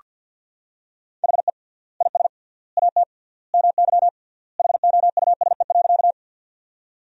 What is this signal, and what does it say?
Morse code, heard as a keyed tone, about 46 wpm, 700 Hz. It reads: HE IS AT MY HOUSE.
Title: 3x Word-Spacing - Sentences from Top 200 Words - 50wpm